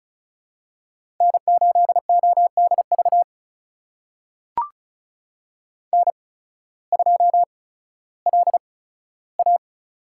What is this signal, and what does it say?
Morse code, heard as a keyed tone, about 36 wpm, spaced as for 16 wpm, 700 Hz. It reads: N8ODV N2LA
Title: USA Call Signs - 8x Character Spacing - 35wpm